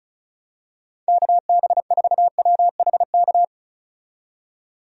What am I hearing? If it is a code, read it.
Morse code, heard as a keyed tone, about 36 wpm, 700 Hz. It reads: KB4WHK
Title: USA Call Signs - 6 Characters - 35wpm